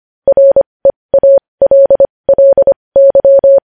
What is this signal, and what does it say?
Morse code, heard as a keyed tone, about 25 wpm, spaced as for 20 wpm, 550 Hz. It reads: REALLY
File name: podcast / News headlines-1x-25x20